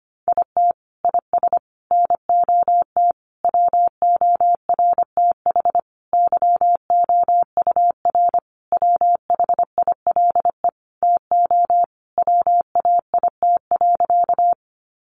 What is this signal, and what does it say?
Morse code, heard as a keyed tone, about 25 wpm, 700 Hz. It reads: IT IS NOT WORTH YOUR WHILE TO WAIT.
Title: Sentences from Top 800 Words 25wpm